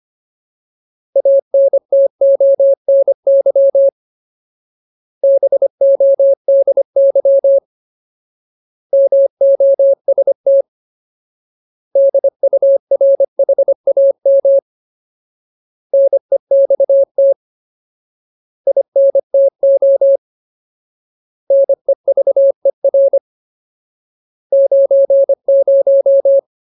Morse code, heard as a keyed tone, about 25 wpm, 550 Hz. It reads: ANTONY BODY MOST DURHAM NEXT INTO NEVER 90
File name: Ultimate Word List / 25WPM Ultimate